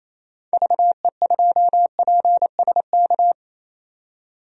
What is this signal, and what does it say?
Morse code, heard as a keyed tone, about 28 wpm, 700 Hz. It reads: VE2PSK
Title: DXCC-100 Call Signs - 6 Characters - 28wpm